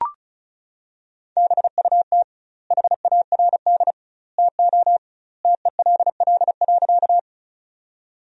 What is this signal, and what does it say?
Morse code, heard as a keyed tone, about 36 wpm, 700 Hz. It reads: BUT HARD TO TELL.
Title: 2x Word-Spacing - Sentences from Top 300 Words - 35wpm